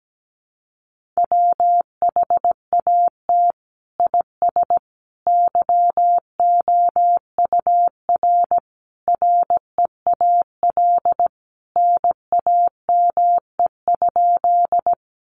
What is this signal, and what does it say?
Morse code, heard as a keyed tone, about 17 wpm, 700 Hz. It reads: WHAT IS YOUR REAL NAME?
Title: Sentences from Top 300 Words 17wpm